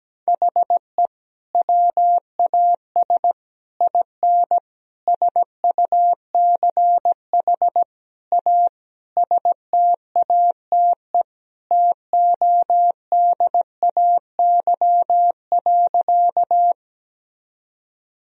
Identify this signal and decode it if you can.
Morse code, heard as a keyed tone, about 17 wpm, 700 Hz. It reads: HE WAS IN SUCH A STATE TODAY.
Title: Sentences from Top 1000 Words - Review Part 2 - 17wpm